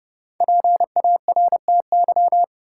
Morse code, heard as a keyed tone, about 30 wpm, 700 Hz. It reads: PARTY
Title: Sentences from Top 1400 Words - New Word Drill - 30wpm